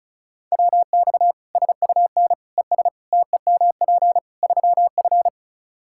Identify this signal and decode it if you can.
Morse code, heard as a keyed tone, about 34 wpm, 700 Hz. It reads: WX SUN ES TEMP 3F
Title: K1USN SST QSO Practice 35wpm